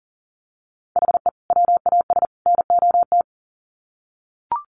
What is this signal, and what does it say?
Morse code, heard as a keyed tone, about 40 wpm, 700 Hz. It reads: HE WAS NOT
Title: Sets of 3 Words 40wpm